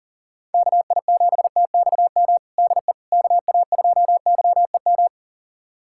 Morse code, heard as a keyed tone, about 40 wpm, 700 Hz. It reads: KI7TXM DE KA2YEM
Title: USA Callsign DE Callsign - 6 Characters - 40wpm